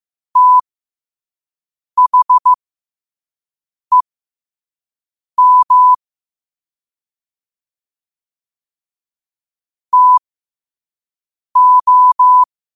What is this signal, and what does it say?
Morse code, heard as a keyed tone, about 15 wpm, spaced as for 5 wpm, 1000 Hz. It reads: THEM TO
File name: qotdfiles / 5wpm